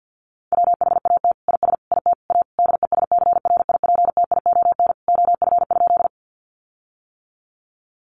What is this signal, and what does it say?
Morse code, heard as a keyed tone, about 50 wpm, 700 Hz. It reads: WHAT IS IT A DESCRIPTION OF?
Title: Sentences from Top 1300 Words 50wpm